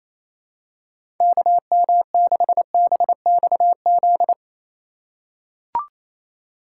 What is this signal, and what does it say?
Morse code, heard as a keyed tone, about 28 wpm, 700 Hz. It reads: KM6BXZ
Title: USA Call Signs - 6 Characters - 3x Character Spacing - 28wpm